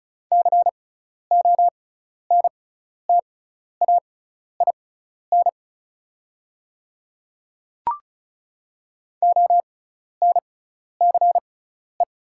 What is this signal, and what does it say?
Morse code, heard as a keyed tone, about 36 wpm, spaced as for 12 wpm, 700 Hz. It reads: CONTAIN ONCE
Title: 6x Character Spacing - Top 500 Words - 35wpm